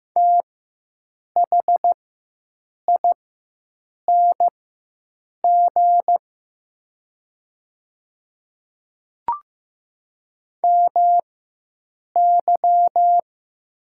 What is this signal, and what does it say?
Morse code, heard as a keyed tone, about 15 wpm, spaced as for 7 wpm, 700 Hz. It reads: THING MY
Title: 4x Character Spacing - Top 100 Words - 15wpm